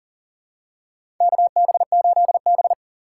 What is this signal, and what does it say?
Morse code, heard as a keyed tone, about 40 wpm, 690 Hz. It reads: KB8B
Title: USA Call Signs - 4 Characters - 2x Character Spacing - 40wpm